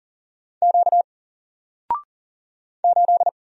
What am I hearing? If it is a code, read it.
Morse code, heard as a keyed tone, about 40 wpm, 700 Hz. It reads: Q 8
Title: Single Letter-Number Alt 40wpm